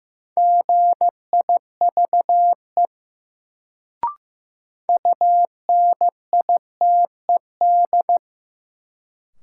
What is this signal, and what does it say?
Morse code, heard as a keyed tone, about 15 wpm, 700 Hz. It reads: GIVE UNITED
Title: Sets of 2 Words - New Word Drill - 15wpm